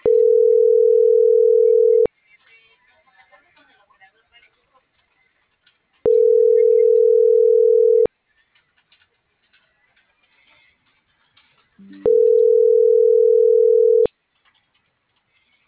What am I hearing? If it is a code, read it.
Ringback tone (North America), 440 + 480 Hz.